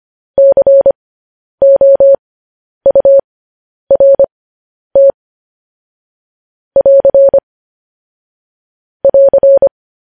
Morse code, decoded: COURT <AR> <AR>

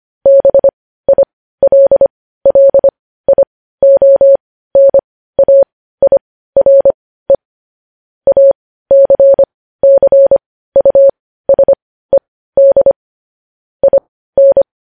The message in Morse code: BILLIONAIRE ACCUSED IN

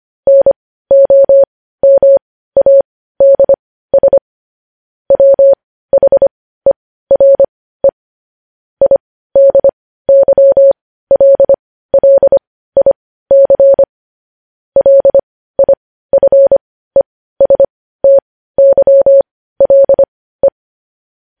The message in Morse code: NOMADS WHERE IDYLLIC LIFESTYLE

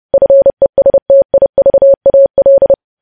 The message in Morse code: FESTIVAL

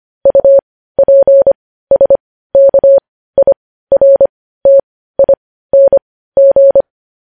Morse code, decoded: UPSKIRTING

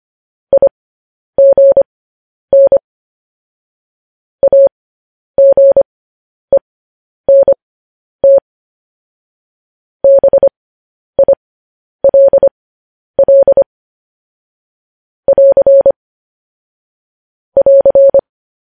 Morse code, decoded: IGN AGENT BILL <AR> <AR>

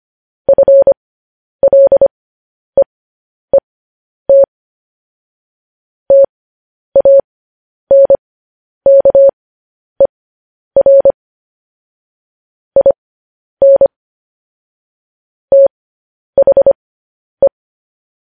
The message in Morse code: FLEET TANKER IN THE

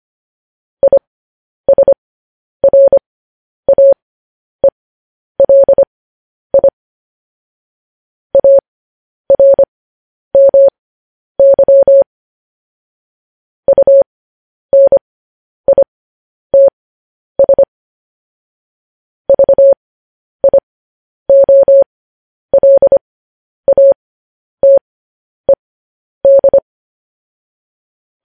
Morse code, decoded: ISRAELI ARMY UNITS VIOLATED

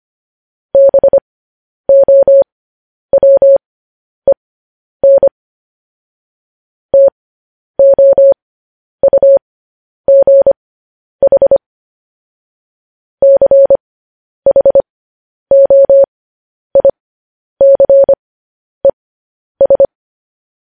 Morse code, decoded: BOWEN TOUGH CHOICES

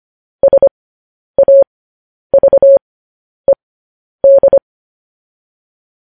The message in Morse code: SAVED

